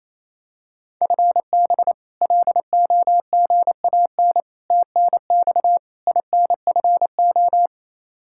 Morse code, heard as a keyed tone, about 28 wpm, 700 Hz. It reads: FB LOGAN TNX INFO
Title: K1USN SST QSO Practice 28wpm